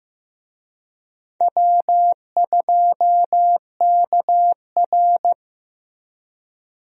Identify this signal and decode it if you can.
Morse code, heard as a keyed tone, about 15 wpm, 700 Hz. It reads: W2KR
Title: USA Call Signs - 4 Characters - Repeated 2x - 15wpm